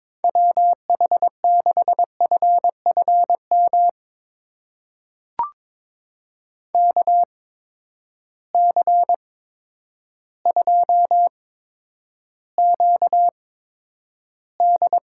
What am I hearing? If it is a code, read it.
Morse code, heard as a keyed tone, about 22 wpm, spaced as for 10 wpm, 700 Hz. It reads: WH6FFM KC2QD
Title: USA Call Signs - 6 Characters - 8x Character Spacing - 22wpm